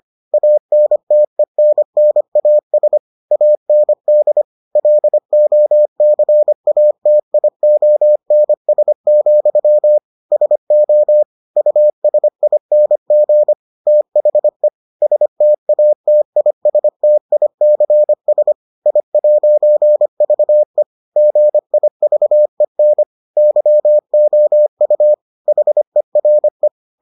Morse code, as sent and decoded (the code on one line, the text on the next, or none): .- -. - . -. -. .- ... / .- -. -.. / .-.. --- -.-. .- - .. --- -. ... --..-- / ... --- / ..- ... .. -. --. / - .... / ... - .- - .. ... - .. -.-. ... / .. .----. ...- . / --. .. ...- . -. / -.-- --- ..- / .... .-. .
ANTENNAS AND LOCATIONS, SO USING THE STATISTICS I'VE GIVEN YOU HERE